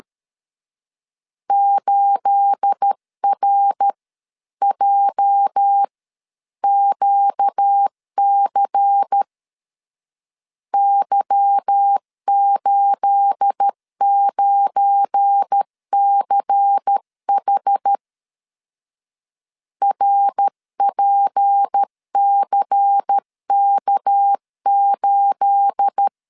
Morse code, heard as a keyed tone, about 13 wpm, 800 Hz. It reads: 8RJQC Y89CH RPCK8
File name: LEZIONI / 9 28 12wpm 2351